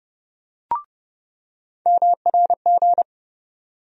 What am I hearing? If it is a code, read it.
Morse code, heard as a keyed tone, about 30 wpm, 700 Hz. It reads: MRG
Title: USA Call Sign Postfixes - 3 Letters 30wpm